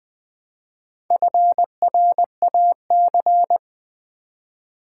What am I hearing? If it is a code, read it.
Morse code, heard as a keyed tone, about 20 wpm, 700 Hz. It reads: FRAC
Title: Build A Word - 2 Letters At A Time - 8-Letter Words - 20wpm